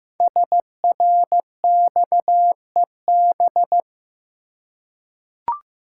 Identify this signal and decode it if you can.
Morse code, heard as a keyed tone, about 15 wpm, 700 Hz. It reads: SRXEB
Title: Code Groups - 5 Letters - 15wpm